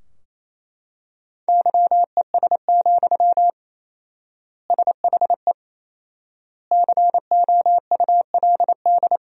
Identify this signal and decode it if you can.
Morse code, heard as a keyed tone, about 28 wpm, 700 Hz. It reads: YES, SHE COULD
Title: 4x Word-Spacing - Sentences from Top 500 Words - 28wpm